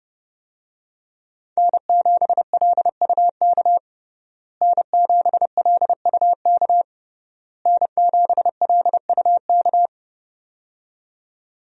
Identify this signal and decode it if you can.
Morse code, heard as a keyed tone, about 30 wpm, 700 Hz. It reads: N7LUK N7LUK N7LUK
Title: USA Call Signs - 5 Characters - Repeated 3x - 30wpm